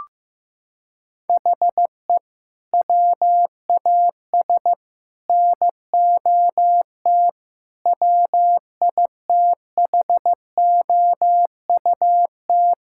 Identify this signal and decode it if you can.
Morse code, heard as a keyed tone, about 16 wpm, 700 Hz. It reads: HE WAS NOT WITHOUT
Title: Sentences from Top 1000 Words - Review Part 6 - 15wpm